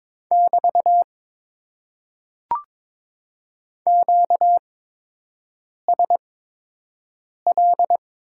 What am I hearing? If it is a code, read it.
Morse code, heard as a keyed tone, about 22 wpm, spaced as for 6 wpm, 700 Hz. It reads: <BT> QSL
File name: Top 90 QSO Elements - 8x Character Spacing - 22wpm